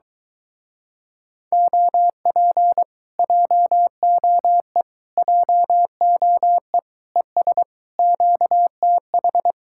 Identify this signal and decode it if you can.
Morse code, heard as a keyed tone, about 23 wpm, 700 Hz. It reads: OP JOE JOE ES QTH